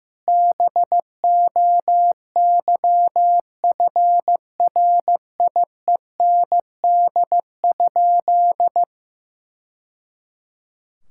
Morse code, heard as a keyed tone, about 15 wpm, 700 Hz. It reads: BOYFRIEND?
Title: Sentences from Top 1000 Words - Review Part 5 - 15wpm